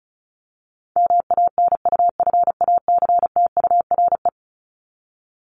Morse code, heard as a keyed tone, about 34 wpm, 700 Hz. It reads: MANUFACTURE
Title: ING Suffix - Top 100 Words with Base Words 35wpm